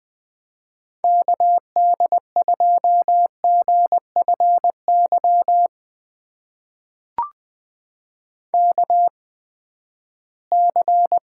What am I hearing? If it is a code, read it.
Morse code, heard as a keyed tone, about 21 wpm, spaced as for 14 wpm, 700 Hz. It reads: KD2GFY KC
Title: USA Call Signs - 6 Characters - 8x Character Spacing - 20wpm